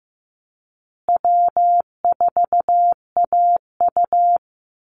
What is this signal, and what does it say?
Morse code, heard as a keyed tone, about 15 wpm, 700 Hz. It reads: W4AU
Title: Easy Call Signs Repeated 2x - 15wpm